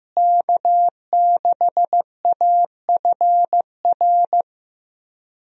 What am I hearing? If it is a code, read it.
Morse code, heard as a keyed tone, about 15 wpm, 700 Hz. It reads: K6AFR
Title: USA Call Signs - 5 Characters - Repeated 3x - 15wpm